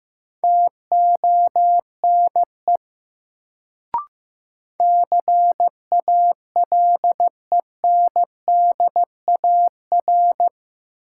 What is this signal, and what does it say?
Morse code, heard as a keyed tone, about 15 wpm, 700 Hz. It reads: TONE CALENDAR
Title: Sentences from Top 1600 Words - New Word Drill Alt - 15wpm